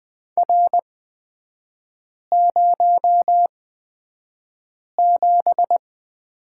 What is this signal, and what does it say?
Morse code, heard as a keyed tone, about 20 wpm, spaced as for 5 wpm, 700 Hz. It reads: R07
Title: 243 Lesson 39e - Letter z - Three Characters 20wpm